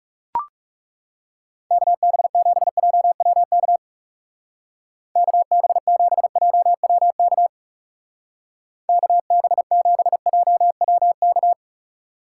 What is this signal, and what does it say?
Morse code, heard as a keyed tone, about 37 wpm, 700 Hz. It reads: KB7JWK KB7JWK KB7JWK
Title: Speed Racing - USA 2x3 Call Signs 30wpm